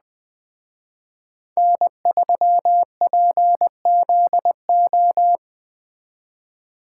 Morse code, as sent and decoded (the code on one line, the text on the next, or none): -. ...-- .--. --.. ---
N3PZO